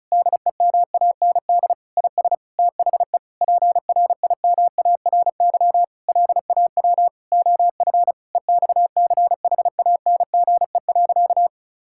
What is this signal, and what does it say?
Morse code, heard as a keyed tone, about 34 wpm, 700 Hz. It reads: DEMAND IS THE PRIMARY LAW OF EXCHANGE.
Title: Sentences from Top 1000 Words - Review Part 6 - 35wpm